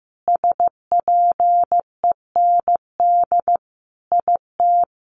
Morse code, heard as a keyed tone, about 15 wpm, 700 Hz. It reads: SPEND IT
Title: Sentences from Top 800 Words 15wpm